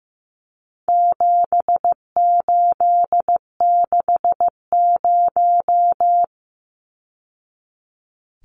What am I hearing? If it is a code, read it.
Morse code, heard as a keyed tone, about 15 wpm, 700 Hz. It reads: 7860
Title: Numbers - 4-Digit 15wpm